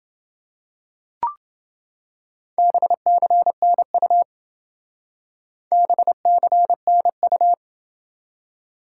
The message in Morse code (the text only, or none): BCNU BCNU